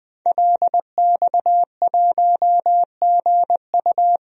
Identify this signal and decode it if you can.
Morse code, heard as a keyed tone, about 20 wpm, 700 Hz. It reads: LX1GU
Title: DXCC-100 Call Signs - 5 Characters - 20wpm